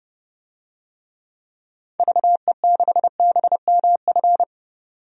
Morse code, heard as a keyed tone, about 32 wpm, 700 Hz. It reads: VE6BMF
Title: DXCC-100 Call Signs - 6 Characters - 30wpm